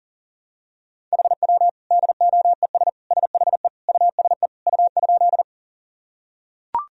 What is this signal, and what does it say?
Morse code, heard as a keyed tone, about 40 wpm, 700 Hz. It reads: HW DOES SHE USE U?
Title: Sentences from Top 200 Words and Abbreviations 40wpm